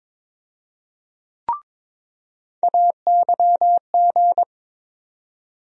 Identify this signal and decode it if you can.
Morse code, heard as a keyed tone, about 22 wpm, 700 Hz. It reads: AYG
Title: USA Call Sign Postfixes - 3 Letters 22wpm